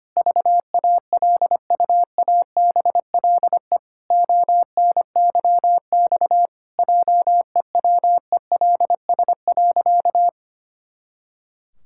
Morse code, heard as a keyed tone, about 25 wpm, 700 Hz. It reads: VALUABLE ONYX JEWELS.
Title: Pangram Challenge v2 - 25wpm